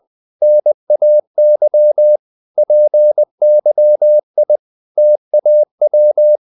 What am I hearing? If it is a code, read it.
Morse code, heard as a keyed tone, about 20 wpm, 600 Hz. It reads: NAY PYI TAW
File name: CW / Practice / NationalCapitals20